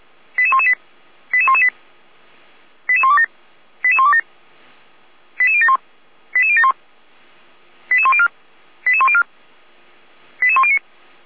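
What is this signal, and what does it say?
Five-tone selective call (ZVEI), heard as quick runs of five tones, tones of about 70 ms. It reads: 88118 88118 88127 88127 88071 88071 88104 88104 88109
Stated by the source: ZVEI five-tone selective call received on SDR.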